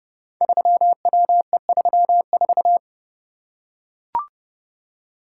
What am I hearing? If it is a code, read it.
Morse code, heard as a keyed tone, about 30 wpm, 700 Hz. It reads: 3WE34
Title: Code Groups - 5 Letter:Numbers - 30wpm